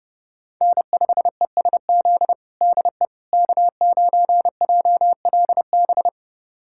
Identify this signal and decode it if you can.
Morse code, heard as a keyed tone, about 30 wpm, 700 Hz. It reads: N5ESZ DE K9JLB
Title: USA Callsign DE Callsign - 5 Characters - 30wpm